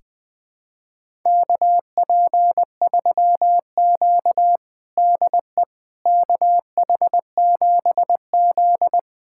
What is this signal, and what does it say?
Morse code, heard as a keyed tone, about 21 wpm, 700 Hz. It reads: KP3Q DE KH7Z
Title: USA Callsign DE Callsign - 4 Characters - 20wpm